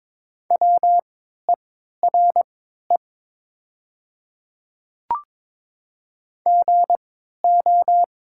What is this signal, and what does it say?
Morse code, heard as a keyed tone, about 22 wpm, spaced as for 12 wpm, 700 Hz. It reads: WERE GO
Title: 3x Character Spacing - Top 100 Words - 22wpm